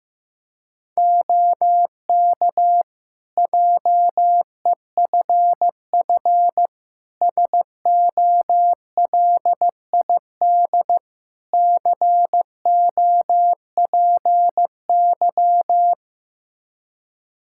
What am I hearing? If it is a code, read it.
Morse code, heard as a keyed tone, about 15 wpm, 700 Hz. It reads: OK JEFF SOLID COPY
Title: K1USN SST QSO Practice 15wpm